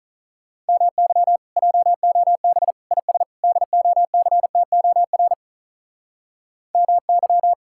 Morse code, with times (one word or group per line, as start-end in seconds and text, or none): 0.7-1.4 MY
1.6-2.7 JOB
2.9-3.2 IS
3.4-5.3 DOCTOR
6.8-7.5 MY